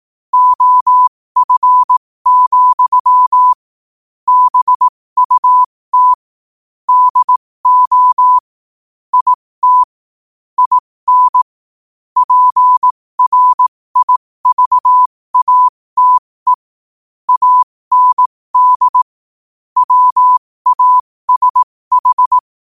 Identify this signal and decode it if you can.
Morse code, heard as a keyed tone, about 18 wpm, spaced as for 15 wpm, 1000 Hz. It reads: OF, BUT DO IT IN PRIVATE AND WASH
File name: qotdfiles / 15wpm